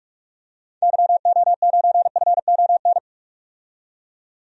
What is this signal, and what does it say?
Morse code, heard as a keyed tone, about 43 wpm, 700 Hz. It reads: YO9FON
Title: DXCC-100 Call Signs - 6 Characters - 45wpm